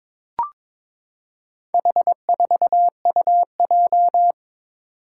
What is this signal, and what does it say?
Morse code, heard as a keyed tone, about 22 wpm, 700 Hz. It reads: H4UJ
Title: Four Challenging Letter-Number Combinations 22wpm